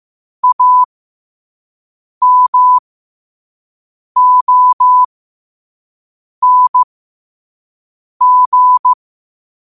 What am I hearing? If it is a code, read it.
Morse code, heard as a keyed tone, about 15 wpm, spaced as for 5 wpm, 1000 Hz. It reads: AMONG